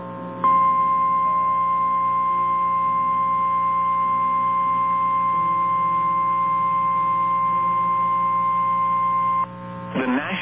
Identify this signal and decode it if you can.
Emergency Alert System (SAME) tones (the 1050 Hz attention tone). It is heard from 0.5 to 9.4 s.
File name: eas / KNCYKBIE / KNCYKBIE